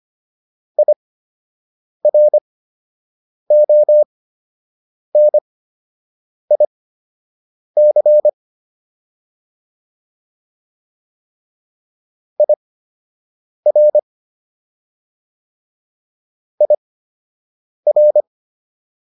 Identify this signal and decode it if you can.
Morse code, heard as a keyed tone, about 25 wpm, spaced as for 7 wpm, 600 Hz. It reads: IRONIC IR IR
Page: fundamental-practice-files